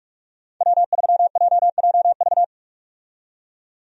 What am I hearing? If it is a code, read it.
Morse code, heard as a keyed tone, about 45 wpm, 700 Hz. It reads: W3JJV
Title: USA Call Signs - 5 Characters - 45wpm